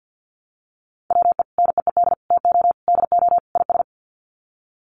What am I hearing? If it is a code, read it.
Morse code, heard as a keyed tone, about 50 wpm, 700 Hz. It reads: WE NEED TO DO IS
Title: Sets of 5 Words Encore2 50wpm